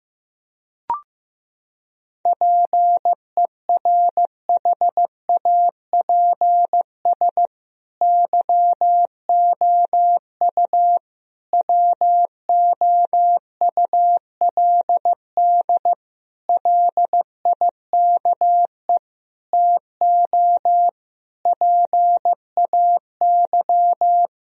Morse code, heard as a keyed tone, about 15 wpm, 700 Hz. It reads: PERHAPS YOU WOULD LIKE TO PAY
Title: Sentences from Top 1000 Words - Review Part 2 - 15wpm